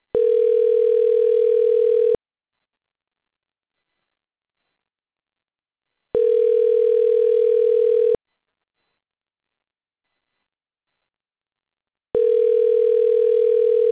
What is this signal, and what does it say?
Ringback tone (North America), 440 + 480 Hz.